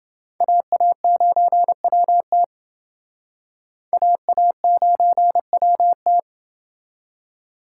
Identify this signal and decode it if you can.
Morse code, heard as a keyed tone, about 28 wpm, 700 Hz. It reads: AA9WT AA9WT